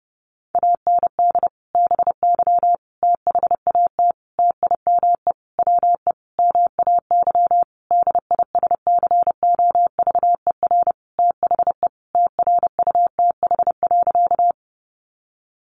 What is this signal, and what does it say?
Morse code, heard as a keyed tone, about 30 wpm, 700 Hz. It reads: AND BY THAT TIME WE MAY DISCOVER THE TRUTH.
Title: Sentences from Top 1100 Words 30wpm